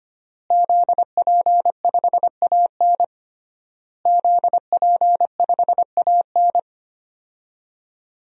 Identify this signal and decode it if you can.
Morse code, heard as a keyed tone, about 25 wpm, 700 Hz. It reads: ZP5AN ZP5AN
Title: DXCC-100 Call Signs - 5 Characters - Repeated 3x - 25wpm